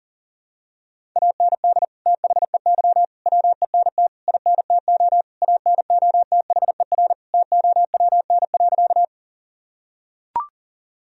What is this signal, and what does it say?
Morse code, heard as a keyed tone, about 40 wpm, 700 Hz. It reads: AND THEY WENT INTO ANOTHER TOWN.
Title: Sentences from Top 1000 Words - Review Part 2 - 40wpm